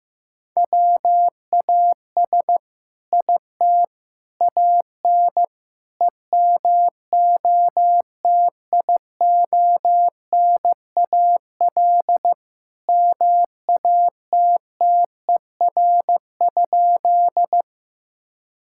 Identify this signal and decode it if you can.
Morse code, heard as a keyed tone, about 15 wpm, 700 Hz. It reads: WAS IT AN EMOTIONAL MATTER?
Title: Sentences from Top 1000 Words - Review Part 5 - 15wpm